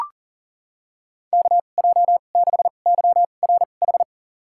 Morse code, heard as a keyed tone, about 40 wpm, spaced as for 29 wpm, 700 Hz. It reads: KJ6YRH
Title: USA Call Signs - 6 Characters - 2x Character Spacing - 40wpm